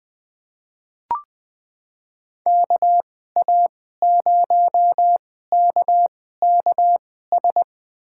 Morse code, heard as a keyed tone, about 20 wpm, spaced as for 14 wpm, 700 Hz. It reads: KA0KKS
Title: USA Call Signs - 6 Characters - 2x Character Spacing - 20wpm